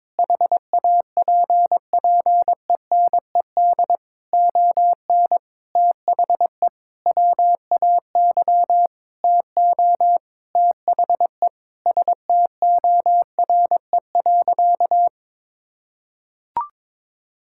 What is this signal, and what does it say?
Morse code, heard as a keyed tone, about 22 wpm, 700 Hz. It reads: HAPPENED ON THE WAY TO THE STORE.
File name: practice / 22wpm / Sentences with 8-Word Phrases - 22wpm